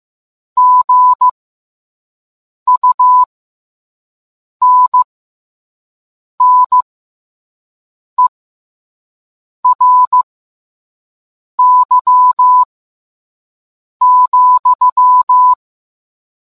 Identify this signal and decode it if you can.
Morse code, heard as a keyed tone, about 15 wpm, spaced as for 5 wpm, 1000 Hz. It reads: GUNNERY,